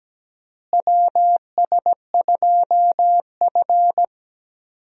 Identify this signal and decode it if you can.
Morse code, heard as a keyed tone, about 17 wpm, 700 Hz. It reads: WS2F
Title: USA Call Signs - 4 Characters - Repeated 3x - 17wpm